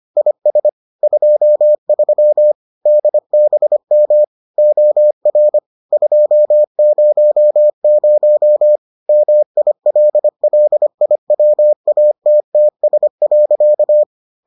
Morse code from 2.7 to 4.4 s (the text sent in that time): DBM